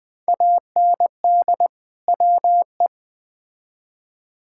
Morse code, heard as a keyed tone, about 20 wpm, 700 Hz. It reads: AND WE
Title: Build A Phrase - 2-Word Phrases - 20wpm